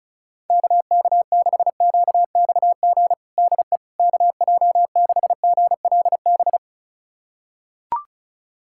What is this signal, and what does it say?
Morse code, heard as a keyed tone, about 36 wpm, 700 Hz. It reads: KK6QXG DE KJ6GLB
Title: USA Callsign DE Callsign - 6 Characters - 35wpm